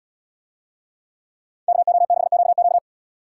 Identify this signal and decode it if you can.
Morse code, heard as a keyed tone, about 75 wpm, 700 Hz. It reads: KO6QQ